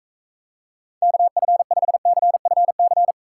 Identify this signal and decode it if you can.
Morse code, heard as a keyed tone, about 45 wpm, 700 Hz. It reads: KF5CFC